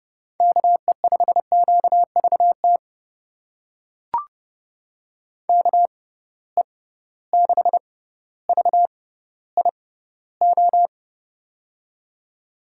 Morse code, heard as a keyed tone, about 30 wpm, spaced as for 15 wpm, 700 Hz. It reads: KE5QVT KE6VIO
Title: USA Call Signs - 6 Characters - 6x Character Spacing - 30wpm